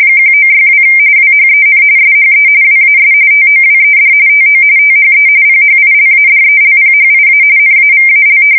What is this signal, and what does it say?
RTTY, about 45.45 baud. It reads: 1962 WHEN SHE WAS TOWED TO THE PHILADELPHIA NAVAL
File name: fsk170 / battleshipnj